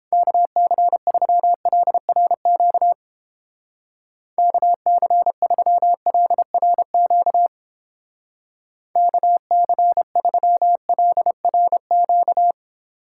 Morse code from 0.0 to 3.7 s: KC3LRQ

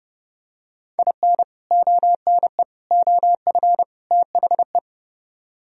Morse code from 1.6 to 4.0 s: ONE OF

